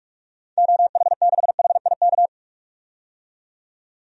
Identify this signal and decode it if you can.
Morse code, heard as a keyed tone, about 46 wpm, 700 Hz. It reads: OH6HIK
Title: DXCC-100 Call Signs - 6 Characters - 45wpm